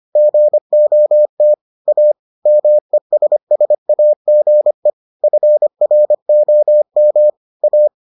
Morse code, heard as a keyed tone, about 25 wpm, 600 Hz. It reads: GOT A MESSAGE FROM A